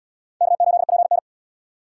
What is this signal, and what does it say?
Morse code, heard as a keyed tone, about 74 wpm, 690 Hz. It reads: K8PN